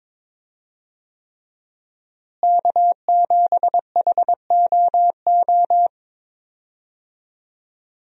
Morse code, heard as a keyed tone, about 22 wpm, 700 Hz. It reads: K7HOO
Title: USA Call Signs - 5 Characters - Repeated 3x - 22wpm